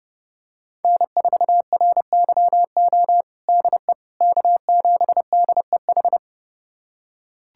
Morse code, heard as a keyed tone, about 30 wpm, 700 Hz. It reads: N4RYO DE K7DEH
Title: USA Callsign DE Callsign - 5 Characters - 30wpm